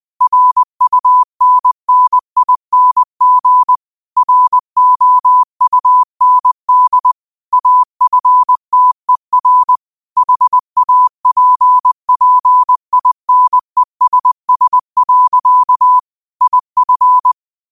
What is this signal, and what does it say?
Morse code, heard as a keyed tone, about 20 wpm, 1000 Hz. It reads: RUNNING ROUND AFTER HAPPINESS. IF